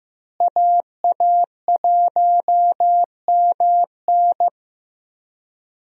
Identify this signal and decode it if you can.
Morse code, heard as a keyed tone, about 14 wpm, 700 Hz. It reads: AA1MN
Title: USA Call Signs - 5 Characters - Repeated 3x - 15wpm